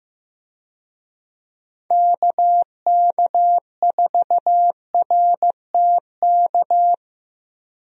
Morse code, heard as a keyed tone, about 15 wpm, 700 Hz. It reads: KK4RTK
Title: USA Call Signs - 6 Characters - Repeated 2x - 15wpm